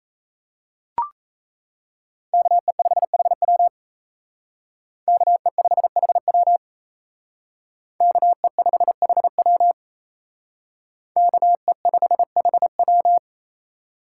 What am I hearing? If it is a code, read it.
Morse code, heard as a keyed tone, about 35 wpm, 700 Hz. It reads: KE5HW KE5HW KE5HW KE5HW
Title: Speed Racing - USA 2x2 Call Signs 28wpm